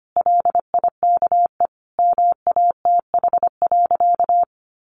Morse code, heard as a keyed tone, about 25 wpm, 700 Hz. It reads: LIKE MATH.
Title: Easy Phrases 25wpm